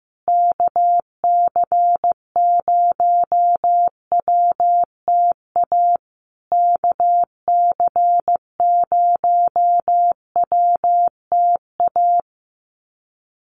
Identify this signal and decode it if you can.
Morse code, heard as a keyed tone, about 15 wpm, 700 Hz. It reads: KC0WTA KC0WTA
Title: USA Call Signs Repeated 3x 15wpm